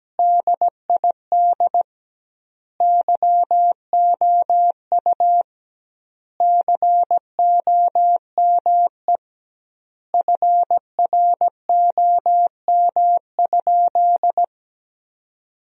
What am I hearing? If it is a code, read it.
Morse code, heard as a keyed tone, about 17 wpm, 700 Hz. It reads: DID YOU COME FROM?
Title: 2x Word-Spacing - Sentences from Top 200 Words - 17wpm